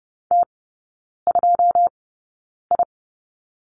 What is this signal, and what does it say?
Morse code, heard as a keyed tone, about 30 wpm, spaced as for 9 wpm, 700 Hz. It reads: T2I
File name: Lesson 16e - Letter c - Three Characters 30wpm